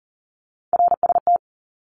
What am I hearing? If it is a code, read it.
Morse code, heard as a keyed tone, about 40 wpm, 700 Hz. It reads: RST
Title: Abbreviations Repeated 3x - 40wpm